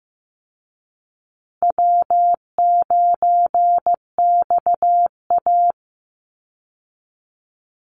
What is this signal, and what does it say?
Morse code, heard as a keyed tone, about 15 wpm, 700 Hz. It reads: W9XA
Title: Easy Call Signs Repeated 2x - 15wpm